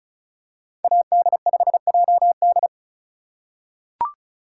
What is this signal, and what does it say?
Morse code, heard as a keyed tone, about 36 wpm, 700 Hz. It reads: AD5JD
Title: USA Call Signs - 5 Characters - 2x Character Spacing - 35wpm